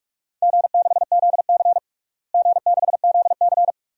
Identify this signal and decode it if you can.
Morse code, heard as a keyed tone, about 44 wpm, 700 Hz. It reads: G6ZC G6ZC
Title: DXCC-100 Call Signs - 4 Characters - Repeated 2x - 45wpm